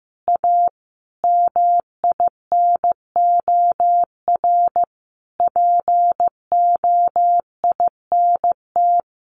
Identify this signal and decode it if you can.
Morse code, heard as a keyed tone, about 15 wpm, 700 Hz. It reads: A MINOR POINT
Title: Sentences from Top 1500 Words 15wpm